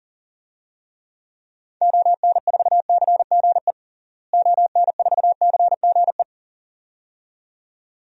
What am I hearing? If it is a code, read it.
Morse code, heard as a keyed tone, about 40 wpm, 700 Hz. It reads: ON4CGE ON4CGE
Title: DXCC-100 Call Signs - 6 Characters - Repeated 2x - 40wpm